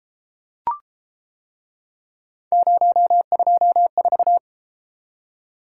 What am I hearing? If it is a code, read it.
Morse code, heard as a keyed tone, about 31 wpm, 700 Hz. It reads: 024